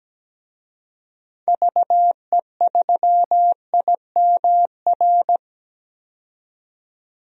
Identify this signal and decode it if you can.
Morse code, heard as a keyed tone, about 17 wpm, 700 Hz. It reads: VE3IMR
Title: DXCC-100 Call Signs - 6 Characters - 17wpm